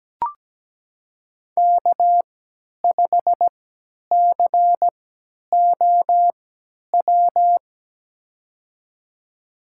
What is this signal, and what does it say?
Morse code, heard as a keyed tone, about 17 wpm, spaced as for 10 wpm, 700 Hz. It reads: K5COW
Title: USA Call Signs - 5 Characters - 3x Character Spacing - 17wpm